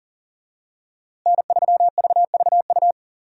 Morse code, heard as a keyed tone, about 37 wpm, 700 Hz. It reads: N3VVU
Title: USA Call Signs - 5 Characters - Repeated 3x - 40wpm